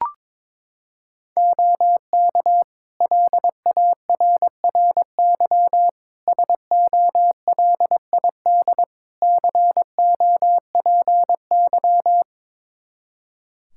Morse code, heard as a keyed tone, about 22 wpm, 700 Hz. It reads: OK LARRY SOLID COPY
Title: Practice QSO - Exchange 3 - More Info - 22wpm